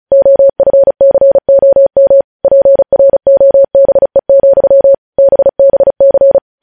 Morse code, heard as a keyed tone, about 36 wpm, 550 Hz. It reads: OFCOM PROBE, BBC
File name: podcast / News headlines-1x-35x35